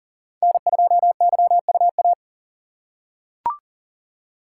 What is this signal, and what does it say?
Morse code, heard as a keyed tone, about 40 wpm, 700 Hz. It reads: N2YUA
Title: USA Call Signs - 5 Characters - 6x Character Spacing - 40wpm